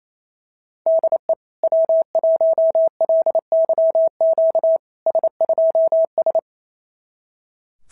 Morse code, heard as a keyed tone, about 28 wpm, 650 Hz. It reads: DE W1LYQ S2S
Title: SOTA QSO Practice 28wpm